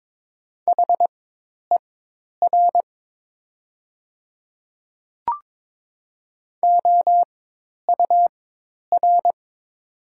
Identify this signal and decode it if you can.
Morse code, heard as a keyed tone, about 22 wpm, spaced as for 10 wpm, 700 Hz. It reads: HER OUR
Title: 4x Character Spacing - Top 200 Words - 22wpm